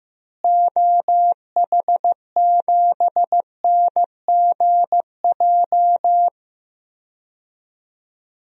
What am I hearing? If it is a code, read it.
Morse code, heard as a keyed tone, about 15 wpm, 700 Hz. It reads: OH7NGJ
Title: DXCC-100 Call Signs - 6 Characters - Repeated 3x - 15wpm